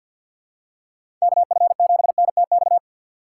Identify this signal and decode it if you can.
Morse code, heard as a keyed tone, about 47 wpm, 700 Hz. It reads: KF7NTX
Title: USA Call Signs - 6 Characters - 4x Character Spacing - 50wpm